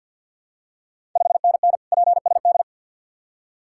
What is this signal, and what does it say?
Morse code, heard as a keyed tone, about 51 wpm, 700 Hz. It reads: 5NN PSD
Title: Ontario QSO Party Exchange 50wpm